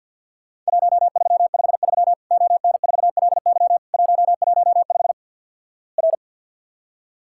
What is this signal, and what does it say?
Morse code, heard as a keyed tone, about 50 wpm, 700 Hz. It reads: 1353 ON4LY 115 R
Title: Worked All Europe DX Contest - QTC Exchange with Repeats - 50wpm